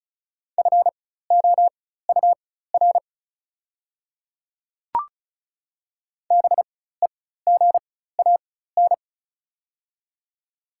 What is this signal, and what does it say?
Morse code, heard as a keyed tone, about 34 wpm, spaced as for 16 wpm, 690 Hz. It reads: FOUR BEGAN
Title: 4x Character Spacing - Top 500 Words - 35wpm